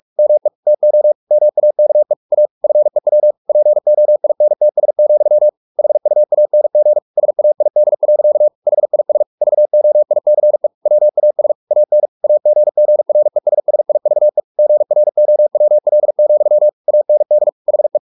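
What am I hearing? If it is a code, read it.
Morse code, heard as a keyed tone, about 44 wpm, 600 Hz. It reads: ME TO MAKE A FEW POINTS, HUANG SAID. HIS VOICE WAS AN AGGRESSIVE GROWL, AND HE
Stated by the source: Morse practice at 45 wpm, the novel Never by Ken Follett in English, from CW med Gnister.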